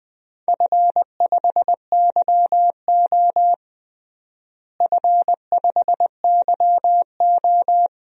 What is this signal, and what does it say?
Morse code, heard as a keyed tone, about 20 wpm, 700 Hz. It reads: F5YO F5YO
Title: DXCC-100 Call Signs - 4 Characters - Repeated 2x - 20wpm